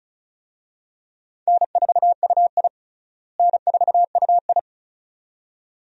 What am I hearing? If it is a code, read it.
Morse code, heard as a keyed tone, about 36 wpm, 700 Hz. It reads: N4UI N4UI
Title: USA Call Signs - 4 Characters - Repeated 2x - 35wpm